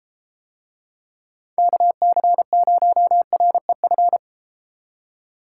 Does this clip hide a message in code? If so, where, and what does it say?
Yes: Morse code from 1.6 to 4.2 s, heard as a keyed tone, about 32 wpm, 700 Hz. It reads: KC0REF